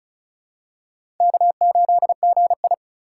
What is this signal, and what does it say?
Morse code, heard as a keyed tone, about 36 wpm, 700 Hz. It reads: K8GI
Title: USA Call Signs - 4 Characters - 2x Character Spacing - 35wpm